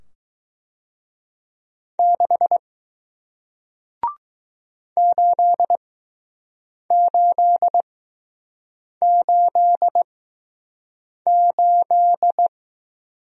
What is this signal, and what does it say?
Morse code, heard as a keyed tone, about 20 wpm, spaced as for 6 wpm, 700 Hz. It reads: 6 8888